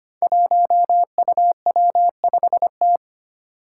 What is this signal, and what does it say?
Morse code, heard as a keyed tone, about 25 wpm, 700 Hz. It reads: 1UW5T
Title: Five Challenging Letter-Number Combinations 25wpm